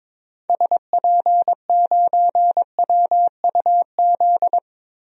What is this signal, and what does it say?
Morse code, heard as a keyed tone, about 22 wpm, 700 Hz. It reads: SP9WUZ
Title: DXCC-100 Call Signs - 6 Characters - 22wpm